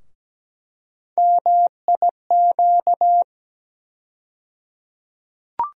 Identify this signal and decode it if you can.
Morse code, heard as a keyed tone, about 17 wpm, 700 Hz. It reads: MIQ
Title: USA Call Sign Postfixes - 3 Letters 17wpm